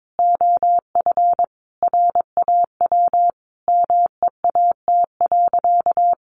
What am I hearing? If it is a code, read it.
Morse code, heard as a keyed tone, about 22 wpm, 700 Hz. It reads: OF RAW MEAT.
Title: Sentences from Top 1500 Words 22wpm